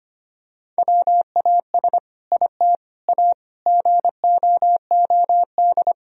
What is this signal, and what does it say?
Morse code, heard as a keyed tone, about 25 wpm, 700 Hz. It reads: WAS IT A GOOD
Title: Sentences from Top 1000 Words - Review Part 4 - 25wpm